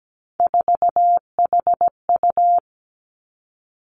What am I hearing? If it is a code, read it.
Morse code, heard as a keyed tone, about 17 wpm, 700 Hz. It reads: 4HU
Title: Three Challenging Letter-Number Combinations 17wpm